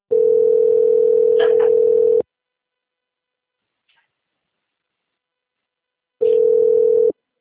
Ringback tone (North America), 440 + 480 Hz.